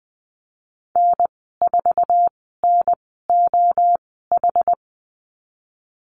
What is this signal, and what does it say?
Morse code, heard as a keyed tone, about 20 wpm, spaced as for 14 wpm, 700 Hz. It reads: N4NOH
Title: 062 Lesson 11g - Letter h - Call Signs 20wpm